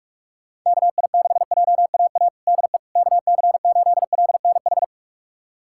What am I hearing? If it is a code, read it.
Morse code, heard as a keyed tone, about 46 wpm, 700 Hz. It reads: KI6JAA DE KC8LNH
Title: USA Callsign DE Callsign 45wpm